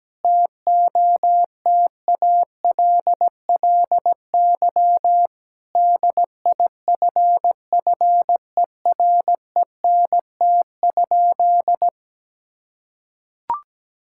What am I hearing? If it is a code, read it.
Morse code, heard as a keyed tone, about 17 wpm, 700 Hz. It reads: TOTALLY DIFFERENT?
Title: Sentences from Top 1600 Words 17wpm